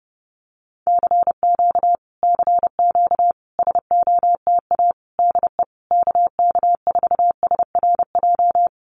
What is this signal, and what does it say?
Morse code, heard as a keyed tone, about 30 wpm, 700 Hz. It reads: CQ CQ SOTA DE KK4SRJ
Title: Common QSO Phrases Encore 30wpm